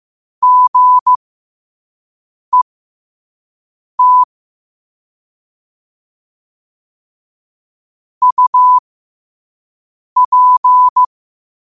Morse code, heard as a keyed tone, about 15 wpm, spaced as for 5 wpm, 1000 Hz. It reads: GET UP
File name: qotdfiles / 5wpm